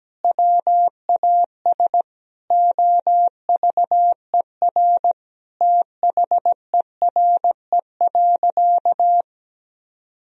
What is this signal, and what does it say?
Morse code, heard as a keyed tone, about 17 wpm, 700 Hz. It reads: WAS OVER THERE.
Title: Sentences from Top 1000 Words - Review Part 3 - 17wpm